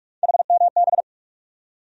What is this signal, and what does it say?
Morse code, heard as a keyed tone, about 46 wpm, 700 Hz. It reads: HMB
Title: USA Call Sign Postfixes - 3 Letters 45wpm